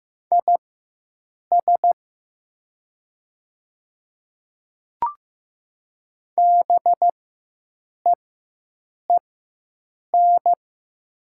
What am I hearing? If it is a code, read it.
Morse code, heard as a keyed tone, about 15 wpm, spaced as for 7 wpm, 700 Hz. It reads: IS BEEN
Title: 4x Character Spacing - Top 100 Words - 15wpm